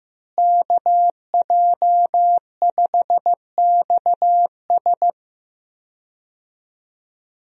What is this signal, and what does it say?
Morse code, heard as a keyed tone, about 16 wpm, 700 Hz. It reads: KJ5XS
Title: USA Call Signs - 5 Characters - Repeated 2x - 15wpm